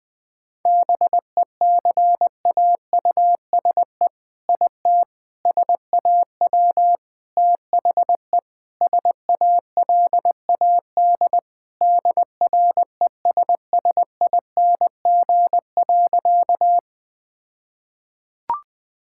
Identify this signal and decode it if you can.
Morse code, heard as a keyed tone, about 20 wpm, 700 Hz. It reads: BECAUSE IT SAW THE SALAD DRESSING.
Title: All-Age Appropriate Jokes - 20wpm